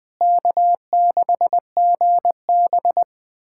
Morse code, heard as a keyed tone, about 21 wpm, 700 Hz. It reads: K6GB
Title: USA Callsign DE Callsign - 4 Characters - 20wpm